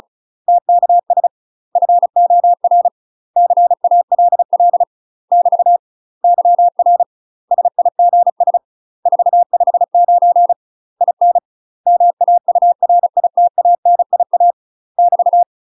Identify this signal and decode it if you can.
Morse code, heard as a keyed tone, about 34 wpm, 700 Hz. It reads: TKS FOR CALL <BT> YR SIGS 459 IN MAURITANIA <BT>
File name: UK MASTER QSOs TRAINING/35WPM